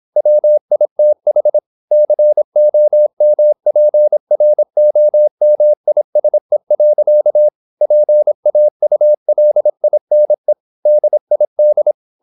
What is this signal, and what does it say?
Morse code, heard as a keyed tone, about 26 wpm, 600 Hz. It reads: WITH COMPROMISE. PAULINE DID
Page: Never 26wpm | CW med Gnister